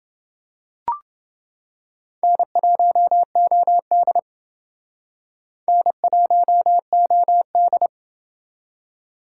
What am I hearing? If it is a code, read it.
Morse code, heard as a keyed tone, about 30 wpm, 700 Hz. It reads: N1OD N1OD